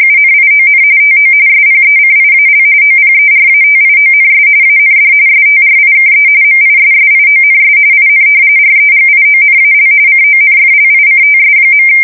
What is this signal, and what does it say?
RTTY, about 45.45 baud. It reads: UNION TO SPECIAL OPERATIONS AND STRIKE MISSIONS IN REGIONAL CONFLICTS.
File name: fsk170 / usnavy5